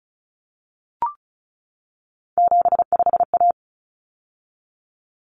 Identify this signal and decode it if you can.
Morse code, heard as a keyed tone, about 34 wpm, 700 Hz. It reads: 75A
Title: Common Cut Numbers - 3-Digit 35wpm